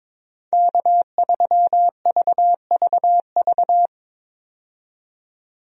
Morse code, heard as a keyed tone, about 22 wpm, 700 Hz. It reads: K3VVV
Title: USA Call Signs - 5 Characters - 22wpm